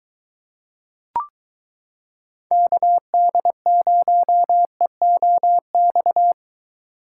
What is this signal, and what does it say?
Morse code, heard as a keyed tone, about 23 wpm, 700 Hz. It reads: KD0EOX